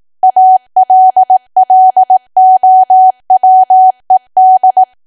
Morse code, heard as a keyed tone, about 18 wpm, 750 Hz. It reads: ALLOWED
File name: cwa / W1AW-Level2 / 18WPM